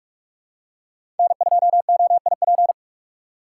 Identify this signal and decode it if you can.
Morse code, heard as a keyed tone, about 46 wpm, 700 Hz. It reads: N2OIP
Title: USA Call Signs - 5 Characters - 3x Character Spacing - 45wpm